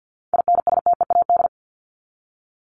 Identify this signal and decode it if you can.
Morse code, heard as a keyed tone, about 50 wpm, 700 Hz. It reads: INSTEAD
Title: Top 501-750 Words 50wpm